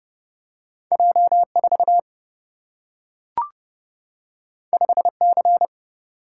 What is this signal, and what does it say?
Morse code, heard as a keyed tone, about 30 wpm, 700 Hz. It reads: J 4 5 C
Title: Single-Letter Single-Number 30wpm